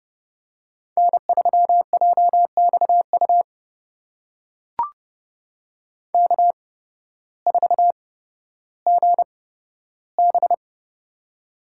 Morse code, heard as a keyed tone, about 32 wpm, spaced as for 14 wpm, 700 Hz. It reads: N3JXU K4GB
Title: USA Call Signs - 8x Character Spacing - 30wpm